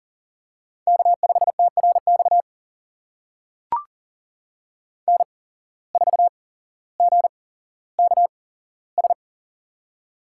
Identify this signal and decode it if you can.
Morse code, heard as a keyed tone, about 40 wpm, spaced as for 17 wpm, 700 Hz. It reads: K5TRX N4GKS
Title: USA Call Signs - 5 Characters - 8x Character Spacing - 40wpm